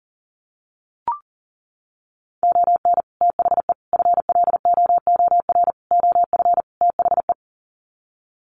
Morse code, heard as a keyed tone, about 40 wpm, 700 Hz. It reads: ON THE FLOOR OF THE
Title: Sets of 5 Words 40wpm